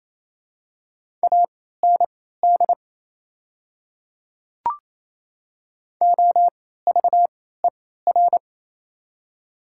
Morse code, heard as a keyed tone, about 28 wpm, spaced as for 16 wpm, 700 Hz. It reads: AND OVER
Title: 3x Character Spacing - Top 300 Words - 28wpm